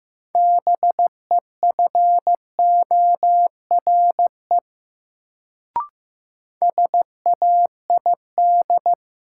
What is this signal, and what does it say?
Morse code, heard as a keyed tone, about 15 wpm, 700 Hz. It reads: BEFORE SAID
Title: Sets of 2 Words - New Word Drill - 15wpm